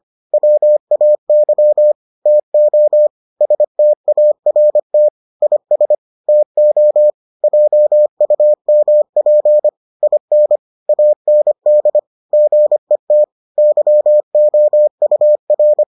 Morse code, heard as a keyed tone, about 25 wpm, 600 Hz. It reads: WAY TO START IS TO JUMP IN AND GET YOUR